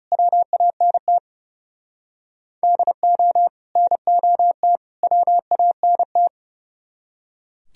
Morse code, heard as a keyed tone, about 35 wpm, 700 Hz. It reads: WANT DO NOT WANT
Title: Speed-Racing - Sets of 3 Words 30wpm